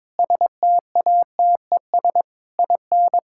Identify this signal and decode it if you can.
Morse code, heard as a keyed tone, about 22 wpm, 700 Hz. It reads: STATES IN
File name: practice / 22wpm / Sentences with 4-Word Phrases - 22wpm